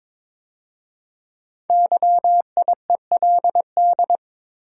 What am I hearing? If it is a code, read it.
Morse code, heard as a keyed tone, about 22 wpm, 700 Hz. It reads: YIELD
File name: 5 Letter Words - First 3 Letters Given - 22wpm